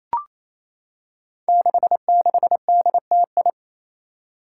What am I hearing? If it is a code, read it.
Morse code, heard as a keyed tone, about 27 wpm, 700 Hz. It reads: 66DTI
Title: Five Challenging Letter-Number Combinations 28wpm